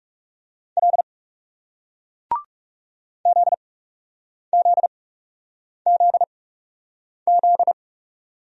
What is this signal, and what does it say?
Morse code, heard as a keyed tone, about 40 wpm, spaced as for 8 wpm, 700 Hz. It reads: L ZZZZ